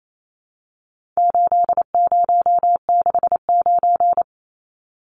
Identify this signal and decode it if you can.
Morse code, heard as a keyed tone, about 28 wpm, 700 Hz. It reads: 8069
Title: Numbers - 4-Digit 28wpm